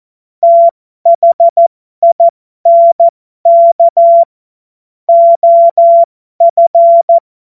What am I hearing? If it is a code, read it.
Morse code, heard as a keyed tone, about 14 wpm, spaced as for 12 wpm, 680 Hz. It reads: THINK OF